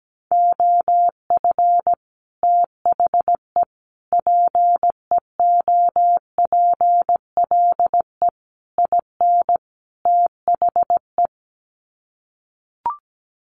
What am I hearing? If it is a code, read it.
Morse code, heard as a keyed tone, about 17 wpm, 700 Hz. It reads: OF THE PEOPLE IN THE
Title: Sets of 5 Words Encore 17wpm